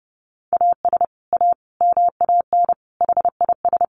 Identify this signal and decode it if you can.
Morse code, heard as a keyed tone, about 30 wpm, 700 Hz. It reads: AS A MAN HIS
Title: Sentences from Top 1200 Words 30wpm